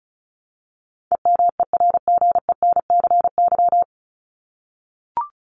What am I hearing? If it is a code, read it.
Morse code, heard as a keyed tone, about 34 wpm, 700 Hz. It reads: EMERGENCY
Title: Top 1501-1750 Words 35wpm